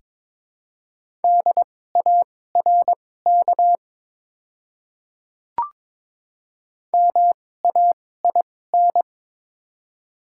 Morse code, heard as a keyed tone, about 23 wpm, spaced as for 16 wpm, 700 Hz. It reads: DARK MAIN
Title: 2x Character Spacing - Top 500 Words - 22wpm